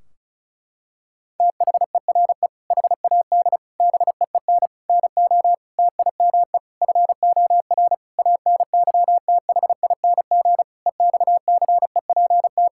Morse code, heard as a keyed tone, about 34 wpm, 700 Hz. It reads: THERE HAD BEEN NO TIME FOR ANYTHING EXCEPT
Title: Sentences from Top 1600 Words 35wpm